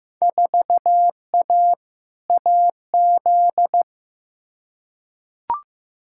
Morse code, heard as a keyed tone, about 15 wpm, 700 Hz. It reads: 4A AZ
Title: Field Day Exchanges - 15wpm